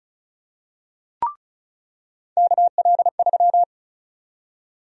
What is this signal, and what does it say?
Morse code, heard as a keyed tone, about 34 wpm, 700 Hz. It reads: KL3
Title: USA Call Sign Prefixes - 35wpm